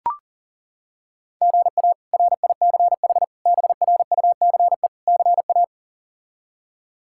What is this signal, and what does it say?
Morse code, heard as a keyed tone, about 40 wpm, 700 Hz. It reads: GA RICH BRUCE CA
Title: K1USN SST QSO Practice - 40wpm